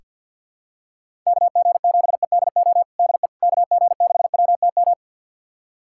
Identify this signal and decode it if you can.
Morse code, heard as a keyed tone, about 48 wpm, 700 Hz. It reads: KG7EDO DE KG6WTM
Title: USA Callsign DE Callsign 50wpm